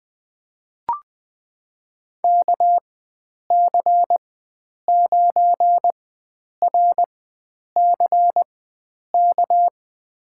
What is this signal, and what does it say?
Morse code, heard as a keyed tone, about 20 wpm, spaced as for 9 wpm, 700 Hz. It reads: KC9RCK
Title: USA Call Signs - 6 Characters - 4x Character Spacing - 20wpm